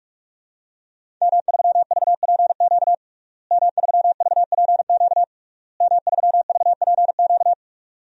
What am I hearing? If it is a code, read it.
Morse code, heard as a keyed tone, about 41 wpm, 700 Hz. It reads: M3VPQ M3VPQ M3VPQ
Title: DXCC-100 Call Signs - 5 Characters - Repeated 3x - 45wpm